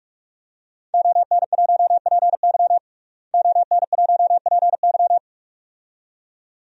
Morse code, heard as a keyed tone, about 44 wpm, 700 Hz. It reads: ON1PY ON1PY
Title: DXCC-100 Call Signs - 5 Characters - Repeated 2x - 45wpm